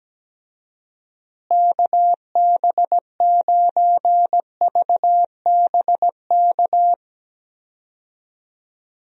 Morse code, heard as a keyed tone, about 17 wpm, 700 Hz. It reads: KB9VBK
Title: USA Call Signs - 6 Characters - Repeated 2x - 17wpm